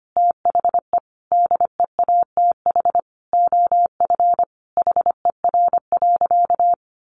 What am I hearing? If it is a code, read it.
Morse code, heard as a keyed tone, about 25 wpm, 700 Hz. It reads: THE DEATH OF HER.
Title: Sentences from Top 1100 Words 25wpm